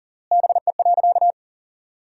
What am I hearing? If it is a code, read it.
Morse code, heard as a keyed tone, about 37 wpm, 700 Hz. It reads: BE.
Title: 3x Word-Spacing - Sentences from Top 200 Words - 40wpm